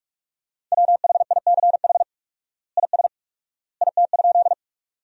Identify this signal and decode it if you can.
Morse code, heard as a keyed tone, about 42 wpm, 700 Hz. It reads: WHICH IS IT?
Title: 4x Word-Spacing - Sentences from Top 100 Words - 45wpm